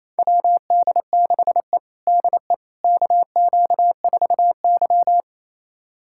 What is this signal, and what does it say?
Morse code, heard as a keyed tone, about 28 wpm, 700 Hz. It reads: WD6E DE KQ4Y
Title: USA Callsign DE Callsign - 4 Characters - 28wpm